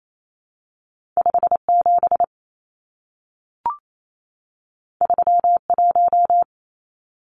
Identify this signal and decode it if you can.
Morse code, heard as a keyed tone, about 28 wpm, 700 Hz. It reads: 57 31
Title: Numbers - 2-Digit 28wpm